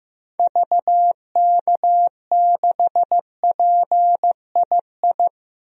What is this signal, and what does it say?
Morse code, heard as a keyed tone, about 15 wpm, 700 Hz. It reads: VK6PII
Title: DXCC-100 Call Signs - 6 Characters - 15wpm